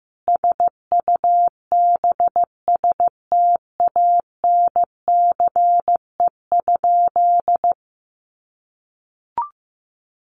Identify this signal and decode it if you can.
Morse code, heard as a keyed tone, about 15 wpm, 700 Hz. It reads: SUBSTANCE?
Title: Sentences from Top 1500 Words 15wpm